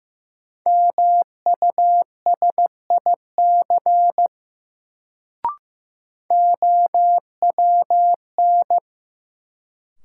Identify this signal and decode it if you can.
Morse code, heard as a keyed tone, about 15 wpm, 700 Hz. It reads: MUSIC OWN
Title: Sentences from Top 300 Words - New Word Drill - 15wpm